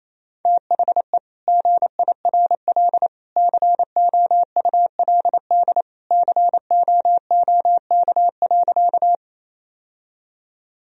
Morse code, heard as a keyed tone, about 27 wpm, 700 Hz. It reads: THE GIRL COULD COOK.
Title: Sentences from Top 1000 Words - Review Part 4 - 28wpm